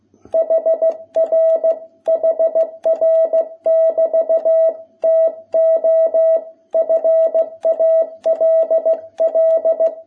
Morse code, heard as a keyed tone, about 16 wpm, 660 Hz. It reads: HR HR<BT>TO FALL
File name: jh8cbh / 2021cw